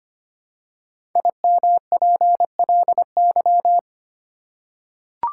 Morse code, heard as a keyed tone, about 25 wpm, 700 Hz. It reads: IMPLY
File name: 5 Letter Words - First 3 Letters Given - 25wpm